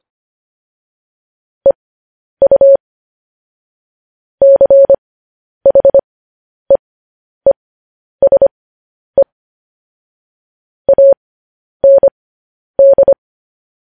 Morse code, heard as a keyed tone, about 25 wpm, spaced as for 10 wpm, 550 Hz. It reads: EU CHEESE AND
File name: podcast / News headlines-1x-25x10